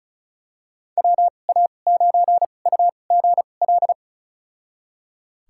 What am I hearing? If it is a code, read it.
Morse code, heard as a keyed tone, about 36 wpm, spaced as for 25 wpm, 700 Hz. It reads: WA9UGL
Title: USA Call Signs - 6 Characters - 2x Character Spacing - 35wpm